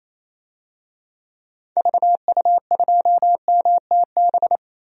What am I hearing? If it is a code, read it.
Morse code, heard as a keyed tone, about 28 wpm, 700 Hz. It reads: VU2MTB
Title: DXCC-100 Call Signs - 6 Characters - Repeated 3x - 28wpm